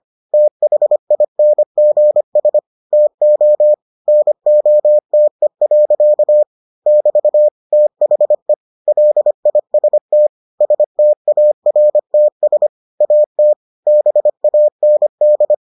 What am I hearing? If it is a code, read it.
Morse code, heard as a keyed tone, about 25 wpm, 600 Hz. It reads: THINGS TO NOTE. <BT> THE LIST STARTS AT BAND